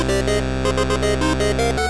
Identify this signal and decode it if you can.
There is no code.